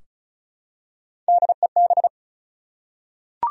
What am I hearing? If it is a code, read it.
Morse code, heard as a keyed tone, about 36 wpm, 700 Hz. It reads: DEB
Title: USA Common Names - Max 4 Letters - 35wpm